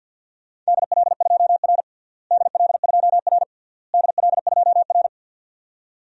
Morse code, heard as a keyed tone, about 53 wpm, 700 Hz. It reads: DL2R DL2R DL2R
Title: DXCC-100 Call Signs - 4 Characters - Repeated 3x - 50wpm